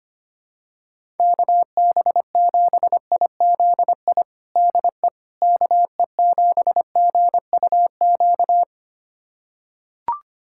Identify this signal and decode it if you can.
Morse code, heard as a keyed tone, about 25 wpm, 700 Hz. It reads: KB7IZI DE KE7GUQ
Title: USA Callsign DE Callsign - 6 Characters - 25wpm